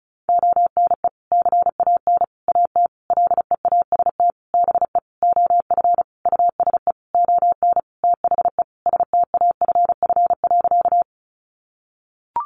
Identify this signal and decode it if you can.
Morse code, heard as a keyed tone, about 34 wpm, 700 Hz. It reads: ONE CAN AT LEAST BE OF USE ON THE STAFF.
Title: Sentences from Top 1200 Words 35wpm